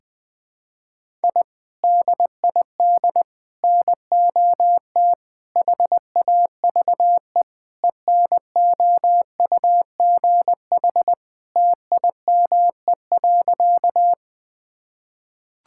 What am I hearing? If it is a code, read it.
Morse code, heard as a keyed tone, about 21 wpm, 700 Hz. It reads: I DID NOT HAVE ENOUGH TIME.
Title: Sentences with 4-Word Phrases - 20wpm